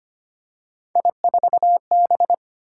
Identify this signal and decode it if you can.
Morse code, heard as a keyed tone, about 25 wpm, 700 Hz. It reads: I4B
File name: Three Letter-Number Combinations - 25wpm